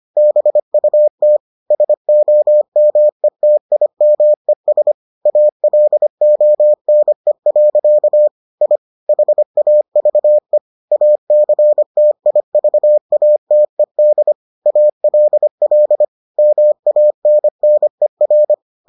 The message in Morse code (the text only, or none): BUT SOMETIMES ALONE. I HAVE ACTIVATED ALL MANNER